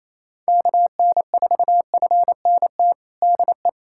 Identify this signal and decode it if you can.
Morse code, heard as a keyed tone, about 27 wpm, 700 Hz. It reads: KN4FNT DE
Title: USA Callsign DE Callsign - 6 Characters - 28wpm